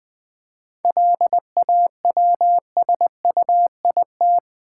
Morse code, heard as a keyed tone, about 20 wpm, 690 Hz. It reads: LAWSUIT
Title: 7 Letter Words - First 3 Letters Given - 20wpm